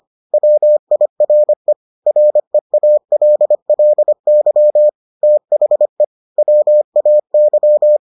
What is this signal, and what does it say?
Morse code, heard as a keyed tone, about 25 wpm, 600 Hz. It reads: WIRE REALLY THE WAY